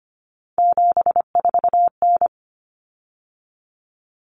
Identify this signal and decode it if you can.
Morse code, heard as a keyed tone, about 25 wpm, 700 Hz. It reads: 74N
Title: Common Cut Numbers - 3-Digit 25wpm